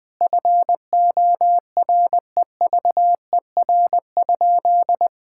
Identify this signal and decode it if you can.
Morse code, heard as a keyed tone, about 20 wpm, 700 Hz. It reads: FOREVER?
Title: Sentences from Top 1600 Words 20wpm